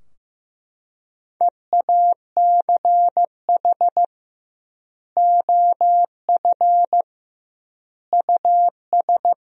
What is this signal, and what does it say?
Morse code, heard as a keyed tone, about 16 wpm, 700 Hz. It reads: EACH OF US
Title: 2x Word-Spacing - Sentences from Top 500 Words - 15wpm